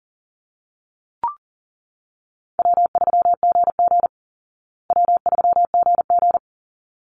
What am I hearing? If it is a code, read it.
Morse code, heard as a keyed tone, about 38 wpm, 700 Hz. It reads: W3GG W3GG
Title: Call Signs Repeated 3x - 40wpm